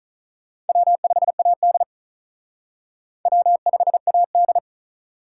Morse code, heard as a keyed tone, about 43 wpm, 700 Hz. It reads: W5AD W5AD
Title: Speed Racing - USA Call Signs 35wpm